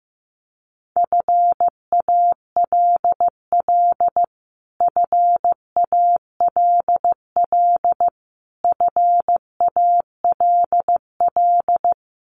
Morse code, heard as a keyed tone, about 15 wpm, 700 Hz. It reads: FALL FALL FALL
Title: Top 500 Words Repeated 15wpm